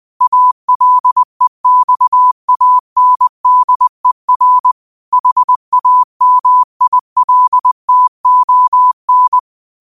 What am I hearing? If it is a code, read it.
Morse code, heard as a keyed tone, about 20 wpm, 1000 Hz. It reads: ALEXANDER HAMILTON
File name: qotdfiles / 20wpm